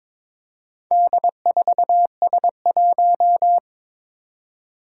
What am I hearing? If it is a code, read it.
Morse code, heard as a keyed tone, about 22 wpm, 700 Hz. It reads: D4S1
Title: Four Challenging Letter-Number Combinations 22wpm